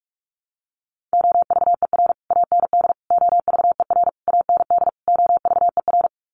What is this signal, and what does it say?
Morse code, heard as a keyed tone, about 44 wpm, 700 Hz. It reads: OVER AND OVER AND OVER
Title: Sets of 5 Words Encore 45wpm